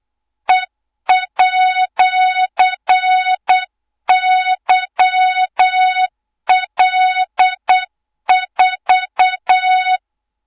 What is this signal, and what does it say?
Morse code, heard as a keyed tone, about 8 wpm, 3700 Hz, with noise. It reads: E@YL4
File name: Morze-HA2MN / hang / 24 GYAKORLAT